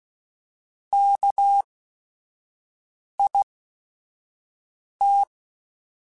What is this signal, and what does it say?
Morse code, heard as a keyed tone, about 16 wpm, spaced as for 5 wpm, 800 Hz. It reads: KIT
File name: Chuckles / Chuckles-mp3s / Guaranteed @5